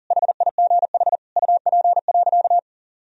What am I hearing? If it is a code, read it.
Morse code, heard as a keyed tone, about 40 wpm, 700 Hz. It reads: HIGH UP.
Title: Sentences from Top 1000 Words - Review Part 1 - 40wpm